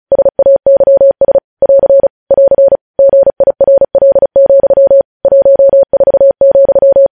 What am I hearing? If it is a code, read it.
Morse code, heard as a keyed tone, about 36 wpm, 550 Hz. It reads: SAYS <AR> <AR> GIRL, 14,
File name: podcast / News headlines-1x-35x35